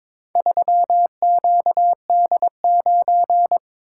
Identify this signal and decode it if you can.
Morse code, heard as a keyed tone, about 22 wpm, 700 Hz. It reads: 3QD9
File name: practice / 22wpm / Four Letter-Number Combinations - 22wpm